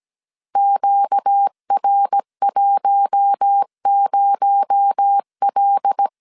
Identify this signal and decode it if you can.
Morse code, heard as a keyed tone, about 17 wpm, 800 Hz. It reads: QR10L
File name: 16wpm